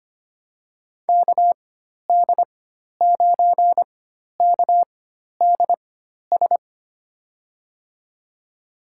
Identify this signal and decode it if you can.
Morse code, heard as a keyed tone, about 25 wpm, spaced as for 12 wpm, 700 Hz. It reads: KD9KDS
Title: USA Call Signs - 6 Characters - 4x Character Spacing - 25wpm